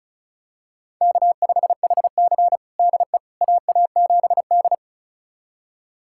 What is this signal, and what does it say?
Morse code, heard as a keyed tone, about 36 wpm, 700 Hz. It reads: K5HC DE AA7D
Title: USA Callsign DE Callsign - 4 Characters - 35wpm